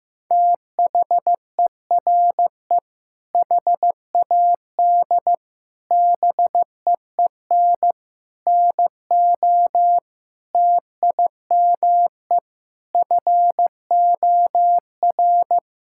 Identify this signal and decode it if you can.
Morse code, heard as a keyed tone, about 15 wpm, 700 Hz. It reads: THERE HAD BEEN NO TIME FOR
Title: Sentences from Top 1600 Words 15wpm